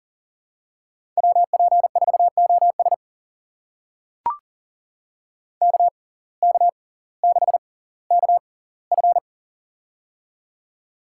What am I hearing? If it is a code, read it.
Morse code, heard as a keyed tone, about 40 wpm, spaced as for 20 wpm, 700 Hz. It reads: WP4OS KK6KF
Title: USA Call Signs - 5 Characters - 6x Character Spacing - 40wpm